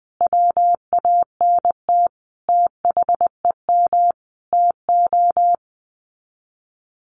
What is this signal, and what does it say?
Morse code, heard as a keyed tone, about 20 wpm, 700 Hz. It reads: WANT THEM TO
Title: Sets of 5 Words Encore 20wpm